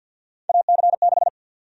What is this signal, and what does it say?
Morse code, heard as a keyed tone, about 49 wpm, 700 Hz. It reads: AC6
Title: USA Call Sign Prefixes - 50wpm